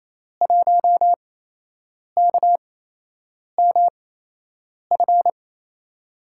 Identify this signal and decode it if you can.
Morse code, heard as a keyed tone, about 29 wpm, spaced as for 8 wpm, 700 Hz. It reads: 1KMF
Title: USA Call Signs - 5 Characters - 8x Character Spacing - 28wpm